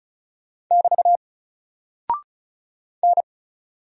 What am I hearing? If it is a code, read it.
Morse code, heard as a keyed tone, about 34 wpm, 700 Hz. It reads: <BT> N
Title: Single Character (Letter:Number:Punctuation:Prosign) 35wpm